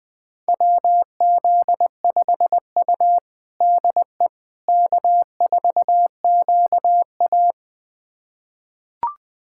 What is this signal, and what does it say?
Morse code, heard as a keyed tone, about 21 wpm, 700 Hz. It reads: WZ5U DE K4QA
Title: USA Callsign DE Callsign - 4 Characters - 20wpm